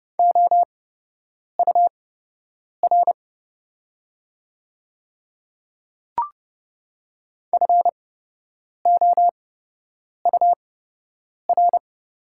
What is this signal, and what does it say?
Morse code, heard as a keyed tone, about 30 wpm, spaced as for 8 wpm, 690 Hz. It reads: OUR FOUR
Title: 8x Character Spacing - Top 300 Words - 30wpm